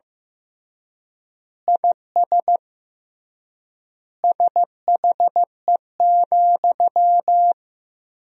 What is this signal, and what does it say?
Morse code, heard as a keyed tone, about 15 wpm, 700 Hz. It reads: IS SHE,
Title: 3x Word-Spacing - Sentences from Top 200 Words - 15wpm